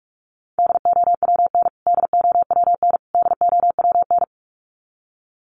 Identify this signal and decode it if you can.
Morse code, heard as a keyed tone, about 44 wpm, 700 Hz. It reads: DOWN DOWN DOWN
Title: Top 300 Words Repeated 45wpm